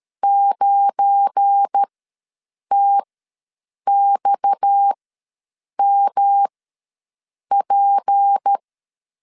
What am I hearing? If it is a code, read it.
Morse code, heard as a keyed tone, about 13 wpm, spaced as for 7 wpm, 800 Hz. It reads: 9TXMP
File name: LEZIONI / 10 SIGNS / 10B INT ws 12wpm 3527